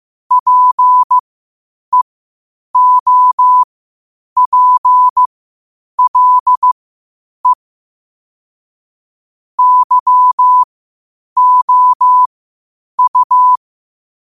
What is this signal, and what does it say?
Morse code, heard as a keyed tone, about 15 wpm, spaced as for 8 wpm, 1000 Hz. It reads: PEOPLE YOU
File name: qotdfiles / 8wpm